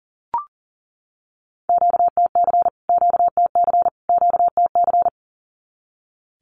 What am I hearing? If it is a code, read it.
Morse code, heard as a keyed tone, about 40 wpm, 700 Hz. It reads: QTC QTC QTC
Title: Q-Codes Repeated 3x 40wpm